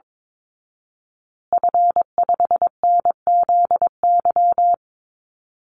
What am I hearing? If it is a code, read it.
Morse code, heard as a keyed tone, about 22 wpm, 700 Hz. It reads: F5NZY
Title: Call Signs Repeated 2x - 22wpm